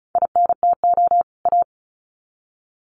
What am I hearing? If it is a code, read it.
Morse code, heard as a keyed tone, about 34 wpm, 700 Hz. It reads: INTO A
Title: Sets of 2 Words 35wpm